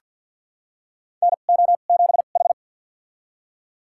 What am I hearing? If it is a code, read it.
Morse code, heard as a keyed tone, about 47 wpm, spaced as for 35 wpm, 700 Hz. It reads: NO7H
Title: USA Call Signs - 4 Characters - 2x Character Spacing - 50wpm